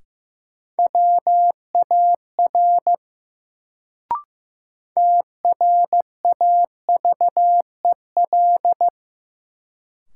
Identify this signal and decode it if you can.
Morse code, heard as a keyed tone, about 15 wpm, 700 Hz. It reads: WAR TRAVEL
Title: Sentences from Top 400 Words - New Word Drill - 15wpm